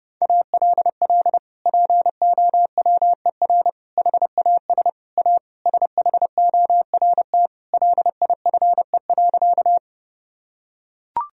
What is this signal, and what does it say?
Morse code, heard as a keyed tone, about 32 wpm, 700 Hz. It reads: ALL POWER HAS A SHORT LIFE.
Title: Sentences from Top 1000 Words - Review Part 2 - 30wpm